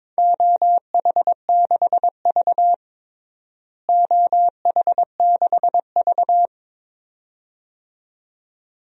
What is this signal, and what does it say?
Morse code, heard as a keyed tone, about 22 wpm, 700 Hz. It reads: OH6V OH6V
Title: DXCC-100 Call Signs - 4 Characters - Repeated 2x - 22wpm